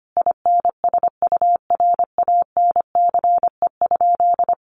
Morse code, heard as a keyed tone, about 25 wpm, 700 Hz. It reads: INSURANCE?
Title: Sentences from Top 1100 Words 25wpm